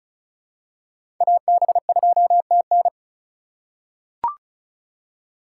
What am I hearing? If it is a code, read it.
Morse code, heard as a keyed tone, about 34 wpm, 700 Hz. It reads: AB2TN
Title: USA Call Signs - 5 Characters - 3x Character Spacing - 35wpm